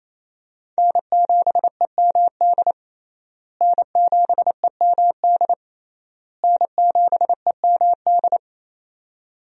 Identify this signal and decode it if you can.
Morse code, heard as a keyed tone, about 28 wpm, 700 Hz. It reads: N7EMD N7EMD N7EMD
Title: USA Call Signs - 5 Characters - Repeated 3x - 28wpm